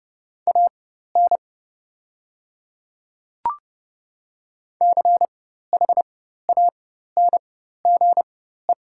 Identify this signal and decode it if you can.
Morse code, heard as a keyed tone, about 30 wpm, spaced as for 14 wpm, 700 Hz. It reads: AN CHANGE
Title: 4x Character Spacing - Top 200 Words - 30wpm